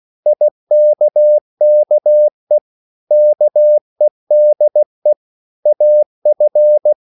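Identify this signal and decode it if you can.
Morse code, heard as a keyed tone, about 16 wpm, 600 Hz. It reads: IKKE KEDE AF
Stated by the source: Morse code at a slow 15 wpm from CW med Gnister.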